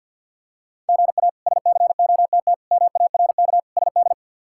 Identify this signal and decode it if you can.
Morse code, heard as a keyed tone, about 52 wpm, 700 Hz. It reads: GA SCOTT MARK SD
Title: K1USN SST QSO Practice - 50wpm